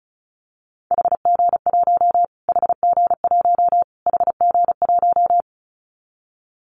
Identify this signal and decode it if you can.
Morse code, heard as a keyed tone, about 34 wpm, 700 Hz. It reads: HG1 HG1 HG1
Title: Callsign Prefixes for Top 25 DXCC - Repeated 3x - 35wpm